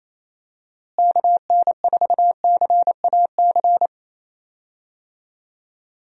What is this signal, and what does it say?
Morse code, heard as a keyed tone, about 28 wpm, 700 Hz. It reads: KN4CAC
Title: USA Call Signs - 6 Characters - Repeated 2x - 28wpm